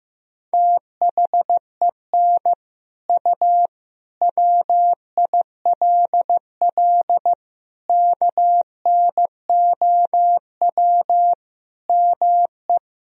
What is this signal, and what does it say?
Morse code, heard as a keyed tone, about 15 wpm, 700 Hz. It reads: THEN U WILL KNOW ME
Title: Sentences from Top 400 Words and Abbreviations 15wpm